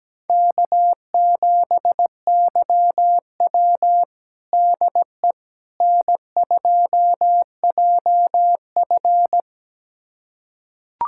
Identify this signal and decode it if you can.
Morse code, heard as a keyed tone, about 17 wpm, 700 Hz. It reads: K7YW DE N2JF
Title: USA Callsign DE Callsign - 4 Characters - 17wpm